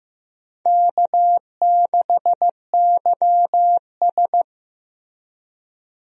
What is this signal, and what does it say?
Morse code, heard as a keyed tone, about 15 wpm, 700 Hz. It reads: K6YS
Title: USA Call Signs - 4 Characters - Repeated 2x - 15wpm